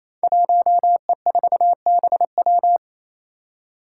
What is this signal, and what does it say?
Morse code, heard as a keyed tone, about 29 wpm, 700 Hz. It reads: 1E4BW
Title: Five Challenging Letter-Number Combinations 28wpm